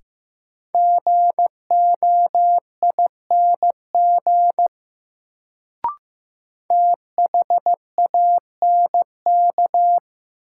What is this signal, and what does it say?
Morse code, heard as a keyed tone, about 15 wpm, 700 Hz. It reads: GOING THANK
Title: Sets of 2 Words - New Word Drill - 15wpm